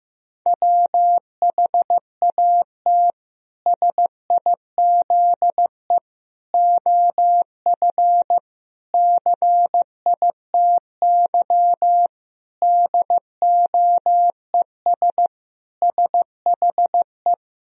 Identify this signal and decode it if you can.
Morse code, heard as a keyed tone, about 16 wpm, 700 Hz. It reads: WHAT SIZE OF CITY DOES SHE
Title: Sentences from Top 1000 Words - Review Part 2 - 15wpm